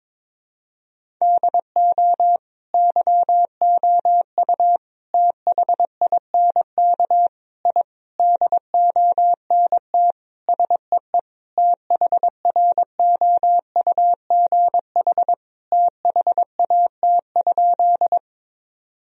Morse code, heard as a keyed tone, about 22 wpm, 700 Hz. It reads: DO YOU THINK I DONT SEE THROUGH THAT?
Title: Sentences from Top 1000 Words - Review Part 1 - 22wpm